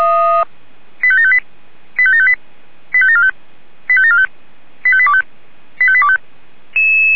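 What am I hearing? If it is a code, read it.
Five-tone selective call (ZVEI), heard as quick runs of five tones, tones of about 70 ms. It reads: 85758 85758 85745 85745 85825 85825
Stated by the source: ZVEI five-tone selective call received via SDR.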